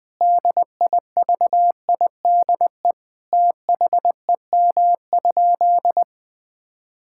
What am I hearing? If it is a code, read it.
Morse code, heard as a keyed tone, about 21 wpm, 700 Hz. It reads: DIVIDE THEM?
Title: Sentences from Top 1600 Words 20wpm